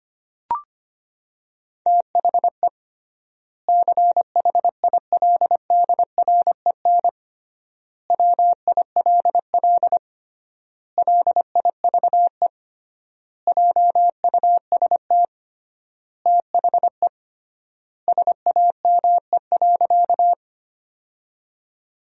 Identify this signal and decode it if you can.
Morse code, heard as a keyed tone, about 25 wpm, 700 Hz. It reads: THE CHILDREN WILL LIVE JUST THE SAME.
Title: 3x Word-Spacing - Sentences from Top 300 Words - 25wpm